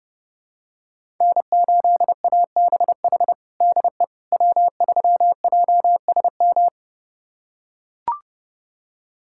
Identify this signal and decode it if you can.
Morse code, heard as a keyed tone, about 30 wpm, 700 Hz. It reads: N8ABH DE W3JSM
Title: USA Callsign DE Callsign - 5 Characters - 30wpm